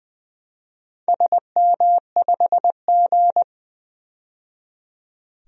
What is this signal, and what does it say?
Morse code, heard as a keyed tone, about 20 wpm, 700 Hz. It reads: SM5G
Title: DXCC-100 Call Signs - 4 Characters - 20wpm